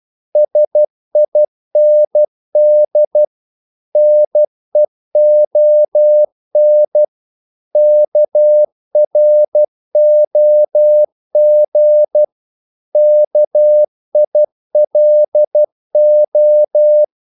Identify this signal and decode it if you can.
Morse code, heard as a keyed tone, about 12 wpm, 600 Hz. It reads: SIND NEON KROG KILO
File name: Korte ord DK 12wpm